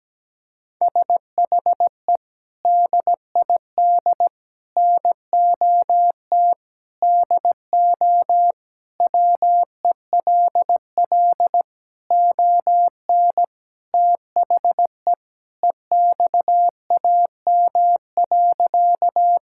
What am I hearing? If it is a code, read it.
Morse code, heard as a keyed tone, about 17 wpm, 700 Hz. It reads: SHE DID NOT DO WELL ON THE EXAM.
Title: Sentences from Top 1000 Words - Review Part 5 - 17wpm